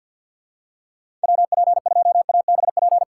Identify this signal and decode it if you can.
Morse code, heard as a keyed tone, about 51 wpm, 700 Hz. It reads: WP2ABP
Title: USA Call Signs - 6 Characters - 4x Character Spacing - 50wpm